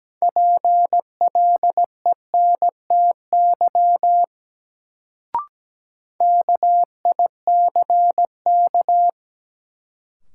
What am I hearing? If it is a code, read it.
Morse code, heard as a keyed tone, about 17 wpm, 700 Hz. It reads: PLENTY KICK
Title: Sentences from Top 1300 Words - New Word Drill - 17wpm